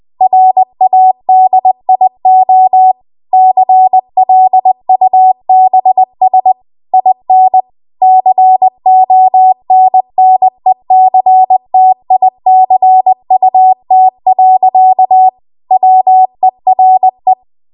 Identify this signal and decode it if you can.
Morse code, heard as a keyed tone, about 20 wpm, 750 Hz. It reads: RADIO CLUBS IN CONNECTICUT. WERE
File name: Morse / 20WPM